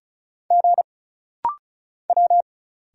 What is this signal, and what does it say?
Morse code, heard as a keyed tone, about 36 wpm, 700 Hz. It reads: G W
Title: Single Letters - Rapid-Fire 35wpm